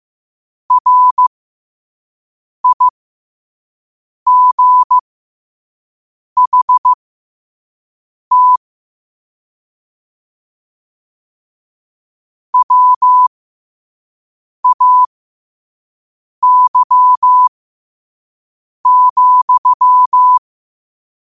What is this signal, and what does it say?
Morse code, heard as a keyed tone, about 15 wpm, spaced as for 5 wpm, 1000 Hz. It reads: RIGHT WAY,
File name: qotdfiles / 5wpm